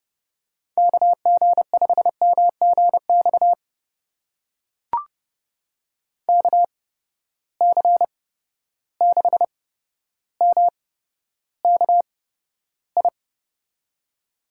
Morse code, heard as a keyed tone, about 30 wpm, spaced as for 13 wpm, 700 Hz. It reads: KG5MGX KC6MKI
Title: USA Call Signs - 6 Characters - 8x Character Spacing - 30wpm